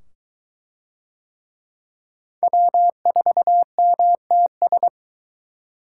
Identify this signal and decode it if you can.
Morse code, heard as a keyed tone, about 22 wpm, 700 Hz. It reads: W4MTS